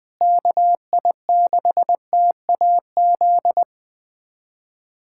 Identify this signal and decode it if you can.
Morse code, heard as a keyed tone, about 20 wpm, 700 Hz. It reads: KI6TAZ
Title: USA Callsign DE Callsign 20wpm